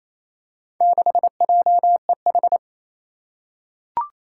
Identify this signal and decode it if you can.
Morse code, heard as a keyed tone, about 28 wpm, 700 Hz. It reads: 6JEH
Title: Four Challenging Letter-Number Combinations 28wpm